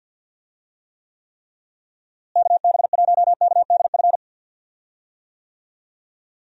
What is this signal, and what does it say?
Morse code, heard as a keyed tone, about 52 wpm, 700 Hz. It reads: KB1KDF
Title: USA Call Signs - 6 Characters - Repeated 3x - 50wpm